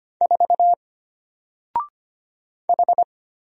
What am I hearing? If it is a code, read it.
Morse code, heard as a keyed tone, about 25 wpm, 700 Hz. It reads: 4 H
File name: Single-Letter-Number-25wpm